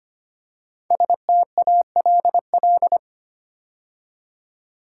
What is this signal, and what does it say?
Morse code, heard as a keyed tone, about 25 wpm, 700 Hz. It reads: STALL
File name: 5 Letter Words - First 3 Letters Given - 25wpm